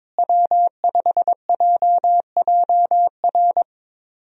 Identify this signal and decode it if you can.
Morse code, heard as a keyed tone, about 22 wpm, 700 Hz. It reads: W5JJR
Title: USA Call Signs - 5 Characters - Repeated 3x - 22wpm